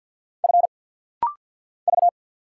Morse code, heard as a keyed tone, about 50 wpm, 700 Hz. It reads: F V
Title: Single Letter-Number - Rapid-Fire 50wpm